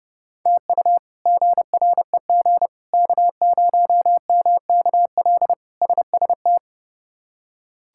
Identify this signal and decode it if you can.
Morse code, heard as a keyed tone, about 30 wpm, 700 Hz. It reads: TU GREG K0MKL SST
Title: K1USN SST QSO Practice - 30wpm